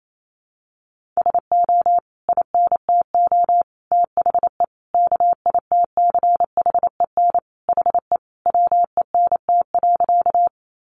Morse code, heard as a keyed tone, about 28 wpm, 700 Hz. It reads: SO INTO THE KITCHEN HE WENT.
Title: Sentences from Top 1400 Words 28wpm